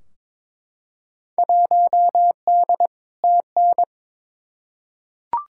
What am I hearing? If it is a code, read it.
Morse code, heard as a keyed tone, about 22 wpm, 700 Hz. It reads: 1D TN
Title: Field Day Exchanges - 22wpm